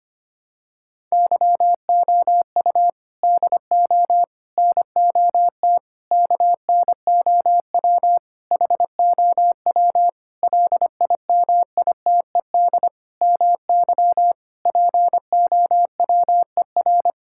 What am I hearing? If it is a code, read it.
Morse code, heard as a keyed tone, about 25 wpm, 700 Hz. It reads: YOU DO NOT KNOW HOW LIMITED MY POWER
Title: Sentences from Top 1000 Words - Review Part 6 - 25wpm